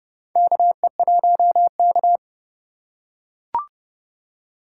Morse code, heard as a keyed tone, about 32 wpm, 700 Hz. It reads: KE1K
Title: USA Call Signs - 4 Characters - 2x Character Spacing - 30wpm